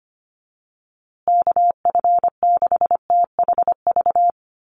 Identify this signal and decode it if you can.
Morse code, heard as a keyed tone, about 25 wpm, 700 Hz. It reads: KF6THV
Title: USA Call Signs Repeated 2x 25wpm